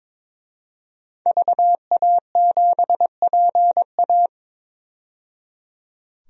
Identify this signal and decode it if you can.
Morse code, heard as a keyed tone, about 22 wpm, 700 Hz. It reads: VA7PA
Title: DXCC-100 Call Signs - 5 Characters - 22wpm